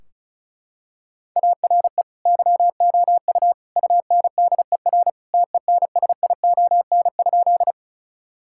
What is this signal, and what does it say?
Morse code, heard as a keyed tone, about 36 wpm, 700 Hz. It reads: ARE YOU UNDER TENSION?
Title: Sentences from Top 1600 Words 35wpm